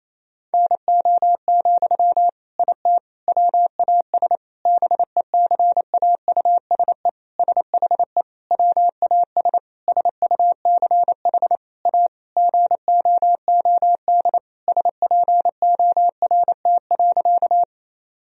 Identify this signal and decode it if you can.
Morse code, heard as a keyed tone, about 28 wpm, 700 Hz. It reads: NO, IT WAS BECAUSE SHE WAS SUCH A GOOD SPORT.
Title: Sentences from Top 1000 Words - Review Part 4 - 28wpm